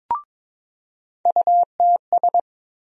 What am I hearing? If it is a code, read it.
Morse code, heard as a keyed tone, about 21 wpm, 700 Hz. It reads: UTS
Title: USA Call Sign Postfixes - 3 Letters 22wpm